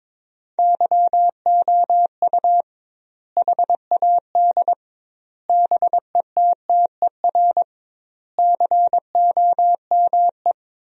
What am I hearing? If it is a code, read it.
Morse code, heard as a keyed tone, about 22 wpm, 700 Hz. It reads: YOU HAD BETTER COME
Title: 2x Word-Spacing - Sentences from Top 400 Words - 22wpm